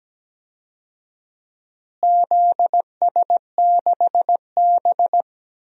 Morse code, heard as a keyed tone, about 17 wpm, 700 Hz. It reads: ZS6B
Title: DXCC-100 Call Signs - 4 Characters - Repeated 3x - 17wpm